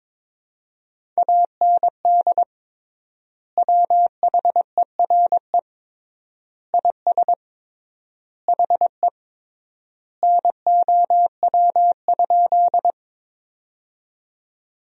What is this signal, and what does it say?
Morse code, heard as a keyed tone, about 22 wpm, 700 Hz. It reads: AND WHERE IS HE NOW?
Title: 3x Word-Spacing - Sentences from Top 200 Words - 22wpm